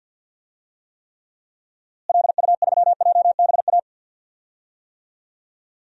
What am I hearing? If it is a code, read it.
Morse code, heard as a keyed tone, about 47 wpm, 700 Hz. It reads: LU3JBA
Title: DXCC-100 Call Signs - 6 Characters - Repeated 2x - 50wpm